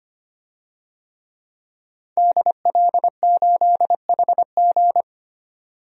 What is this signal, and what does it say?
Morse code, heard as a keyed tone, about 25 wpm, 700 Hz. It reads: DL8HG
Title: DXCC-100 Call Signs - 5 Characters - 25wpm